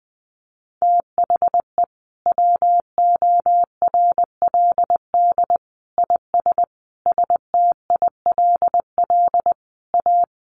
Morse code, heard as a keyed tone, about 20 wpm, 700 Hz. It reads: THE WORLD IS STILL A
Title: Sentences from Top 1200 Words 20wpm